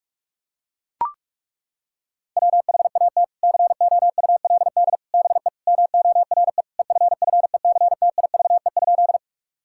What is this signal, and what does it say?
Morse code, heard as a keyed tone, about 42 wpm, 700 Hz. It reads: WHAT COULD BE MORE EFFECTIVE?
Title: Sentences from Top 1000 Words - Review Part 6 - 45wpm